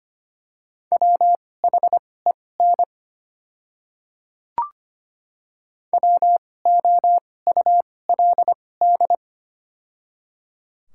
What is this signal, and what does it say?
Morse code, heard as a keyed tone, about 25 wpm, spaced as for 18 wpm, 700 Hz. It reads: WHEN WOULD
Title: 2x Character Spacing - Top 100 Words - 25wpm